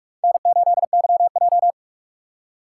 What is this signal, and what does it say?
Morse code, heard as a keyed tone, about 46 wpm, 700 Hz. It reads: N8YJ
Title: USA Call Signs - 4 Characters - 6x Character Spacing - 45wpm